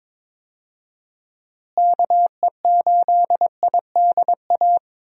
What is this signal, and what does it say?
Morse code, heard as a keyed tone, about 22 wpm, 700 Hz. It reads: KE8IDA
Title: USA Call Signs - 6 Characters - Repeated 3x - 22wpm